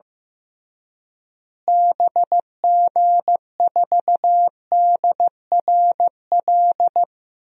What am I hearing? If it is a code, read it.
Morse code, heard as a keyed tone, about 15 wpm, 700 Hz. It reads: BG4DRL
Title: DXCC-100 Call Signs - 6 Characters - Repeated 3x - 15wpm